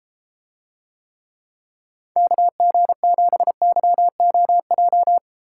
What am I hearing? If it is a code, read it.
Morse code, heard as a keyed tone, about 32 wpm, 700 Hz. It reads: KG7YOJ